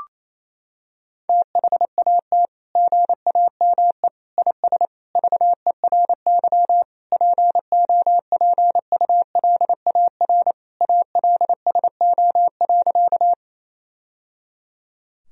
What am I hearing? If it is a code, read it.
Morse code, heard as a keyed tone, about 28 wpm, 700 Hz. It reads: THAT GAME IS VERY POPULAR ALSO.
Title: Sentences from Top 1000 Words - Review Part 3 - 28wpm